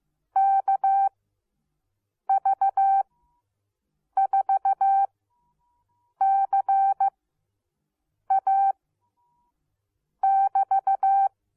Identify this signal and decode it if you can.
Morse code, heard as a keyed tone, about 15 wpm, spaced as for 6 wpm, 780 Hz. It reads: KV4CA <BT>